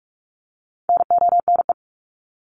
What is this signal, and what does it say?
Morse code, heard as a keyed tone, about 44 wpm, 700 Hz. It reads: NONE
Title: Top 1501-1750 Words 45wpm